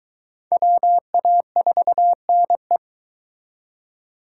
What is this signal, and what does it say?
Morse code, heard as a keyed tone, about 22 wpm, 700 Hz. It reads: WA4NE